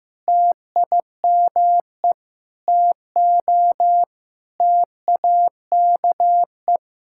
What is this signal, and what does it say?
Morse code, heard as a keyed tone, about 15 wpm, 700 Hz. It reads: TIME TO TAKE
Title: Sentences from Top 1000 Words - Review Part 5 - 15wpm